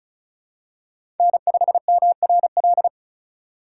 Morse code, heard as a keyed tone, about 36 wpm, 700 Hz. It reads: N5MRL
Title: USA Call Signs - 5 Characters - 2x Character Spacing - 35wpm